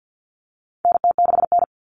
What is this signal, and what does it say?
Morse code, heard as a keyed tone, about 50 wpm, 700 Hz. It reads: NT6N
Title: All Cut Numbers - 4-Digit 50wpm